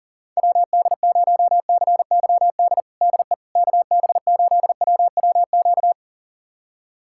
Morse code, heard as a keyed tone, about 40 wpm, 700 Hz. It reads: WD0CYD DE KB8WWQ
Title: USA Callsign DE Callsign - 6 Characters - 40wpm